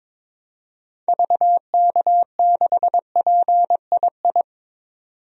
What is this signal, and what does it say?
Morse code, heard as a keyed tone, about 22 wpm, 700 Hz. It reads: VK6PII
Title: DXCC-100 Call Signs - 6 Characters - 22wpm